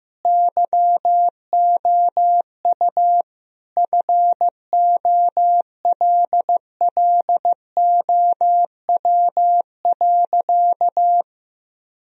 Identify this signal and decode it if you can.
Morse code, heard as a keyed tone, about 15 wpm, 700 Hz. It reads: YOU FOLLOW.
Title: Sentences from Top 1000 Words - Review Part 3 - 15wpm